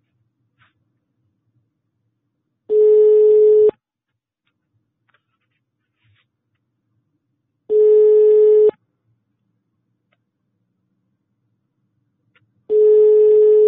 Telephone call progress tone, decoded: ringback tone